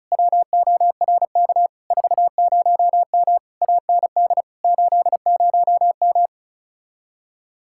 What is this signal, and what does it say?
Morse code, heard as a keyed tone, about 37 wpm, 700 Hz. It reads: WORK 40M AND 80M
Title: Speed-Racing Common QSO Phrases - 30wpm